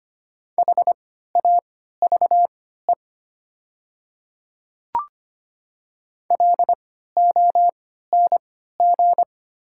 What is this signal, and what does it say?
Morse code, heard as a keyed tone, about 25 wpm, spaced as for 14 wpm, 700 Hz. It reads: HAVE LONG
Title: 3x Character Spacing - Top 100 Words - 25wpm